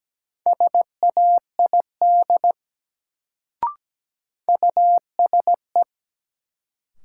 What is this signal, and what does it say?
Morse code, heard as a keyed tone, about 17 wpm, 700 Hz. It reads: SAID USE
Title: Sentences from Top 100 Words - New Word Drill - 17wpm